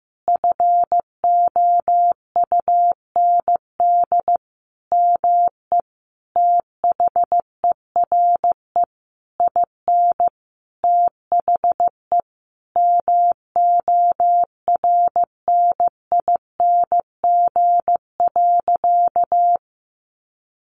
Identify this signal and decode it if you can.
Morse code, heard as a keyed tone, about 15 wpm, 700 Hz. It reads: FOUND ME THERE IN THE MORNING.
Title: Sentences from Top 1200 Words 15wpm